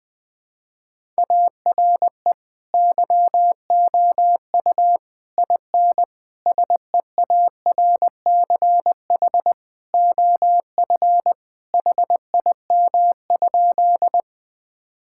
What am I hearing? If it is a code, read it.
Morse code, heard as a keyed tone, about 21 wpm, 700 Hz. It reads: ARE YOU IN SEARCH OF HIM?
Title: Sentences from Top 1000 Words - Review Part 5 - 20wpm